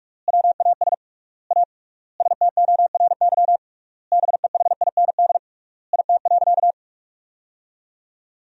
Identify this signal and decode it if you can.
Morse code, heard as a keyed tone, about 46 wpm, 700 Hz. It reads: WAS A STORY BEHIND IT.
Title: 3x Word-Spacing - Sentences from Top 500 Words - 45wpm